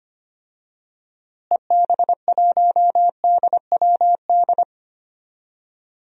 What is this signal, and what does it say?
Morse code, heard as a keyed tone, about 25 wpm, 700 Hz. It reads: EB1DWD
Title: DXCC-100 Call Signs - 6 Characters - Repeated 3x - 25wpm